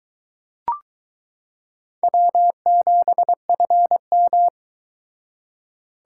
Morse code, heard as a keyed tone, about 22 wpm, 700 Hz. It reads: W7FM